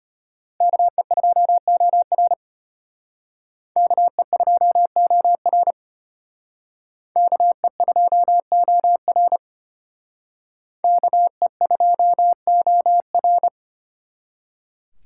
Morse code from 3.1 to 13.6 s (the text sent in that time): KE2OR KE2OR KE2OR